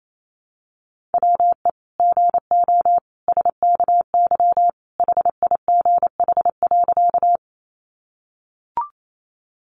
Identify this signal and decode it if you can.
Morse code, heard as a keyed tone, about 28 wpm, 700 Hz. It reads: WE GO SKY HIGH.
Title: Sentences from Top 1200 Words 28wpm